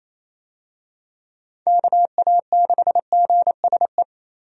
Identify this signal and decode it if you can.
Morse code, heard as a keyed tone, about 28 wpm, 700 Hz. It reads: KA6GSE
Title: USA Call Signs - 6 Characters - Repeated 3x - 28wpm